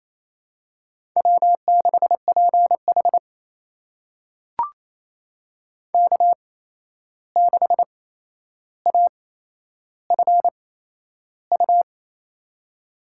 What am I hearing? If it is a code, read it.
Morse code, heard as a keyed tone, about 27 wpm, spaced as for 11 wpm, 700 Hz. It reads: W6PH K6AFU
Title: USA Call Signs - 8x Character Spacing - 28wpm